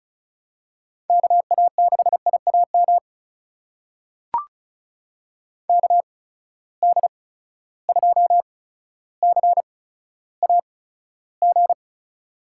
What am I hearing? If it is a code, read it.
Morse code, heard as a keyed tone, about 36 wpm, spaced as for 15 wpm, 700 Hz. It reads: KA6IAM KD2CAG
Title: USA Call Signs - 6 Characters - 8x Character Spacing - 35wpm